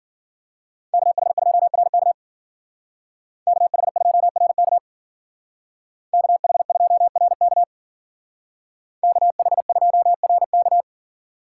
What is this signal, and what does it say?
Morse code, heard as a keyed tone, about 52 wpm, 700 Hz. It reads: KH2RK KH2RK KH2RK KH2RK